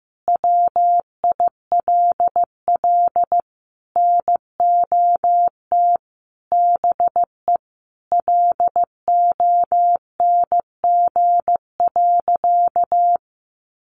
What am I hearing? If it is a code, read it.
Morse code, heard as a keyed tone, about 15 wpm, 700 Hz. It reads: WILL NOT BE LONG.
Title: Sentences from Top 1200 Words 15wpm